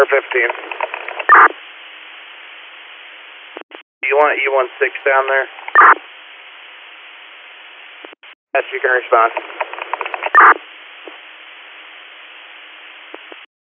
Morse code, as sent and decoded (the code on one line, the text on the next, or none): none